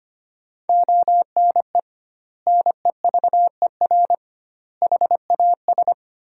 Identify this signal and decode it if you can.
Morse code, heard as a keyed tone, about 25 wpm, 700 Hz. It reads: ONE NEVER HAS
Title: 2x Word-Spacing - Sentences from Top 300 Words - 25wpm